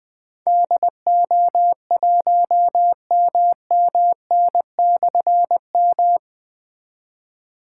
Morse code, heard as a keyed tone, about 20 wpm, 700 Hz. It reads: DO1MMN/M
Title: Difficult Call Signs 20wpm